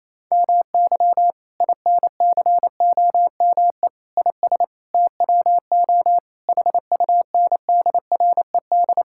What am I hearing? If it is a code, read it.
Morse code, heard as a keyed tone, about 28 wpm, 700 Hz. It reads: MY INCOME IS TWO HUNDRED
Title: Sentences from Top 1000 Words - Review Part 5 - 28wpm